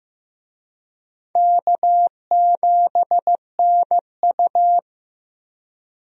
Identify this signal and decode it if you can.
Morse code, heard as a keyed tone, about 15 wpm, 700 Hz. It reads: K7NU
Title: USA Call Signs - 4 Characters - Repeated 3x - 15wpm